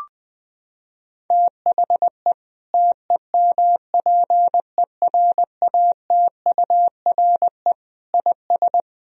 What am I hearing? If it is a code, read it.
Morse code, heard as a keyed tone, about 20 wpm, 700 Hz. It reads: THE TEMPERATURE IS
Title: Sentences from Top 1000 Words - Review Part 4 - 20wpm